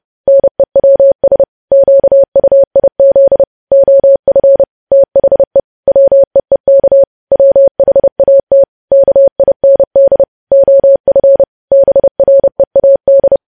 Morse code, heard as a keyed tone, about 30 wpm, 550 Hz. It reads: NEWS QUIZ OF THE WEEK WHAT KIND OF BREAD
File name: podcast / News headlines-1x-30x30